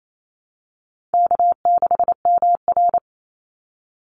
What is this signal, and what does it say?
Morse code, heard as a keyed tone, about 28 wpm, 700 Hz. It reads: K6MR
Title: Easy Call Signs Repeated 3x - 28wpm